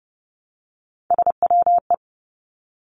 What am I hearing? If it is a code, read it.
Morse code, heard as a keyed tone, about 30 wpm, 700 Hz. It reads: SWE
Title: Three Challenging Letter-Number Combinations 30wpm